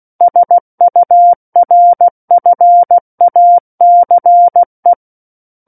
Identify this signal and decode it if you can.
Morse code, heard as a keyed tone, about 16 wpm, 700 Hz. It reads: SURFACE
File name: Book One / 07-Chapter 7 - 16 WPM